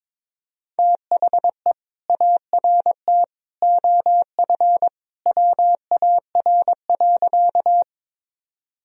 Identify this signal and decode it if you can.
Morse code, heard as a keyed tone, about 22 wpm, 700 Hz. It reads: THE ART OF WAR.
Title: Sentences from Top 1000 Words - Review Part 3 - 22wpm